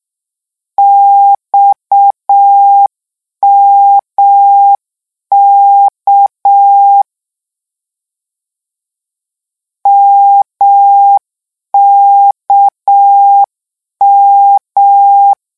Morse code, heard as a keyed tone, about 6 wpm, 800 Hz. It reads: XMK MKM
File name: morse mp3 / lecon 1-vitesse 6